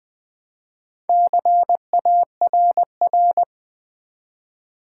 Morse code, heard as a keyed tone, about 20 wpm, 700 Hz. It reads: CARR
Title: Build A Word - 1 Letter At A Time - 5-Letter Words - 20wpm